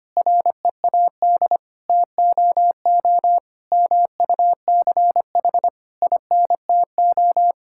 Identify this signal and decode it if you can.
Morse code, heard as a keyed tone, about 25 wpm, 700 Hz. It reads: READ TOO MUCH INTO
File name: practice / 25wpm / Sentences with 9-Word Phrases - 25wpm